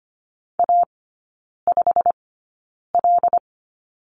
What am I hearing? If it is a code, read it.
Morse code, heard as a keyed tone, about 25 wpm, spaced as for 9 wpm, 690 Hz. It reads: A5L
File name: Lesson 15e - Number 5 - Three Characters 25wpm